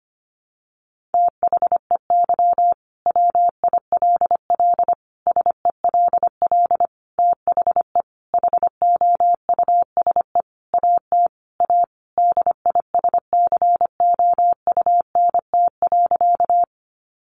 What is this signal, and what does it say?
Morse code, heard as a keyed tone, about 25 wpm, 700 Hz. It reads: THEY WILL SELL THE HOUSE AT A DISCOUNT.
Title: Sentences from Top 1200 Words 25wpm